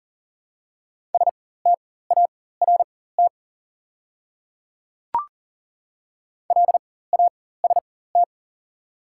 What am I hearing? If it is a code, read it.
Morse code, heard as a keyed tone, about 41 wpm, spaced as for 19 wpm, 700 Hz. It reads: START LAST
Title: 4x Character Spacing - Top 300 Words - 40wpm